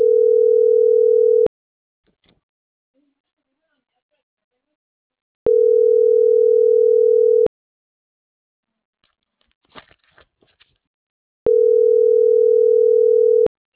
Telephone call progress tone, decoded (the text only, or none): ringback tone